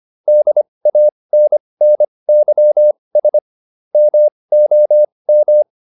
Morse code, heard as a keyed tone, about 25 wpm, spaced as for 20 wpm, 600 Hz. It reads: DANNYS MOM